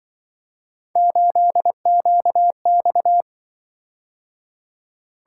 Morse code, heard as a keyed tone, about 25 wpm, 700 Hz. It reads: 8QX